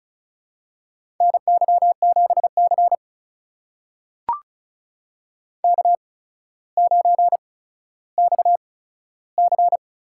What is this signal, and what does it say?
Morse code, heard as a keyed tone, about 34 wpm, spaced as for 15 wpm, 700 Hz. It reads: NY7C K9XC
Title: USA Call Signs - 4 Characters - 8x Character Spacing - 35wpm